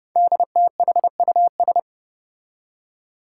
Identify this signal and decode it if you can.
Morse code, heard as a keyed tone, about 28 wpm, 700 Hz. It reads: DTHUS
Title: Five Challenging Letter-Number Combinations 30wpm